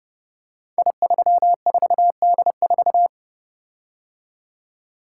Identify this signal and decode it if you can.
Morse code, heard as a keyed tone, about 30 wpm, 700 Hz. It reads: I34D4
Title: Code Groups - 5 Letter:Numbers - 30wpm